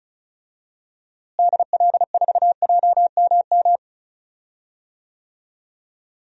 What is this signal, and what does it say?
Morse code, heard as a keyed tone, about 34 wpm, 700 Hz. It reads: DL4JMM
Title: DXCC-100 Call Signs - 6 Characters - Repeated 2x - 35wpm